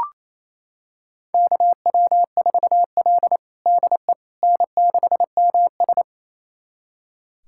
Morse code, heard as a keyed tone, about 28 wpm, 700 Hz. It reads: KW4L DE N6MS
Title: USA Callsign DE Callsign - 4 Characters - 28wpm